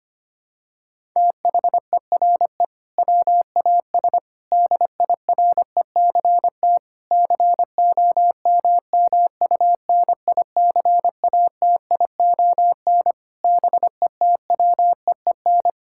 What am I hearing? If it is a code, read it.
Morse code, heard as a keyed tone, about 25 wpm, 700 Hz. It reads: THERE WAS DIRECT COMMUNICATION BETWEEN
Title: Sentences from Top 1000 Words - Review Part 6 - 25wpm